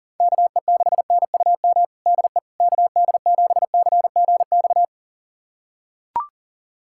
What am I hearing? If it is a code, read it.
Morse code, heard as a keyed tone, about 37 wpm, 700 Hz. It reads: KE6NUM DE KD7CGX
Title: USA Callsign DE Callsign - 6 Characters - 40wpm